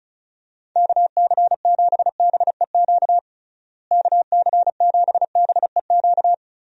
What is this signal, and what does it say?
Morse code, heard as a keyed tone, about 36 wpm, 700 Hz. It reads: KC7BEQ KC7BEQ
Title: USA Call Signs - 6 Characters - Repeated 2x - 35wpm